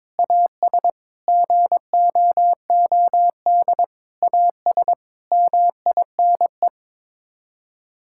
Morse code, heard as a keyed tone, about 22 wpm, 700 Hz. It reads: AS GOOD AS MINE
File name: practice / 22wpm / Sets of 7 Words - 22wpm